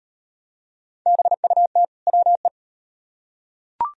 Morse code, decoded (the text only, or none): BUT WE